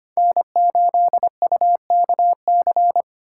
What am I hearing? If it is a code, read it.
Morse code, heard as a keyed tone, about 25 wpm, 700 Hz. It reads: N8UKC
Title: USA Call Signs - 4x Character Spacing - 25wpm